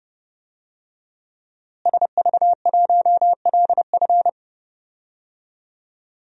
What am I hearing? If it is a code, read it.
Morse code, heard as a keyed tone, about 30 wpm, 700 Hz. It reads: SV1LF
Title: DXCC-100 Call Signs - 5 Characters - Repeated 3x - 30wpm